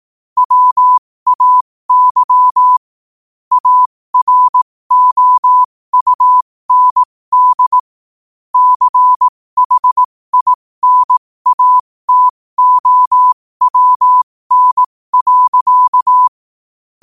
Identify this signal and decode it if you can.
Morse code, heard as a keyed tone, about 18 wpm, spaced as for 15 wpm, 1000 Hz. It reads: WAY AROUND CHINATOWN.
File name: qotdfiles / 15wpm